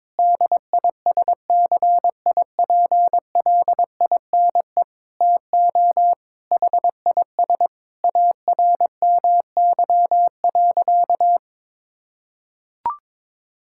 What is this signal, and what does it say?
Morse code, heard as a keyed tone, about 22 wpm, 700 Hz. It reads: DISCIPLINE TO HIS ARMY.
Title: Sentences from Top 1000 Words - Review Part 6 - 22wpm